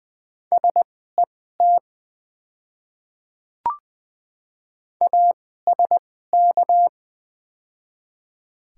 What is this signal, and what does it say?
Morse code, heard as a keyed tone, about 20 wpm, spaced as for 14 wpm, 700 Hz. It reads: SET ASK
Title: 2x Character Spacing - Top 200 Words - 20wpm